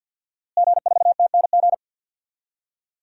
Morse code, heard as a keyed tone, about 51 wpm, 700 Hz. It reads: G4TNG
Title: DXCC-100 Call Signs - 5 Characters - Repeated 2x - 50wpm